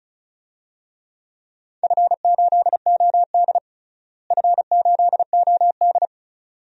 Morse code, heard as a keyed tone, about 36 wpm, 700 Hz. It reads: F8OD F8OD
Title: DXCC-100 Call Signs - 4 Characters - Repeated 2x - 35wpm